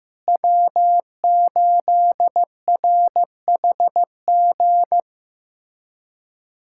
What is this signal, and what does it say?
Morse code, heard as a keyed tone, about 15 wpm, 700 Hz. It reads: W8RHG
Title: USA Callsign DE Callsign - 5 Characters - 15wpm